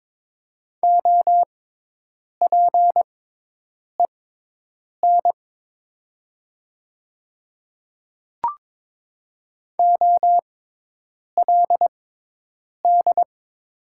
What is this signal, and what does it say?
Morse code, heard as a keyed tone, about 22 wpm, spaced as for 8 wpm, 700 Hz. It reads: OPEN OLD
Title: 6x Character Spacing - Top 500 Words - 22wpm